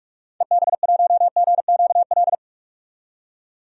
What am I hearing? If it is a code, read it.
Morse code, heard as a keyed tone, about 44 wpm, 690 Hz. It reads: EB1GQL
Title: DXCC-100 Call Signs - 6 Characters - 45wpm